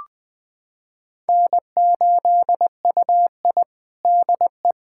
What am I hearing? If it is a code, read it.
Morse code, heard as a keyed tone, about 20 wpm, 700 Hz. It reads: N8UI DE
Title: USA Callsign DE Callsign 20wpm